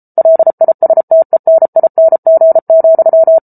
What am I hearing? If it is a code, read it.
Morse code, heard as a keyed tone, about 34 wpm, 650 Hz. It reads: LISTENING,